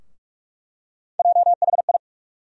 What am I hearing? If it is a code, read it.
Morse code, heard as a keyed tone, about 46 wpm, 700 Hz. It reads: JHI
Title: USA Call Sign Postfixes - 3 Letters 45wpm